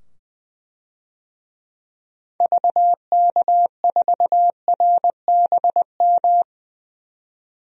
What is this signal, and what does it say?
Morse code, heard as a keyed tone, about 20 wpm, 700 Hz. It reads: VK4RBM
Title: DXCC-100 Call Signs - 6 Characters - 20wpm